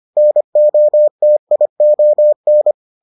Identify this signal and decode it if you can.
Morse code, heard as a keyed tone, about 25 wpm, 600 Hz. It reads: NOTION